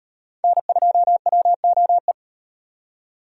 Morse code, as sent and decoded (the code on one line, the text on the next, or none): -. ..--- .-- --- .
N2WOE